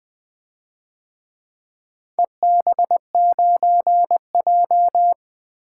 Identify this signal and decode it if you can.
Morse code, heard as a keyed tone, about 25 wpm, 700 Hz. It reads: EB9J
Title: Speed-Racing - Four Letter-Number Combinations - 20wpm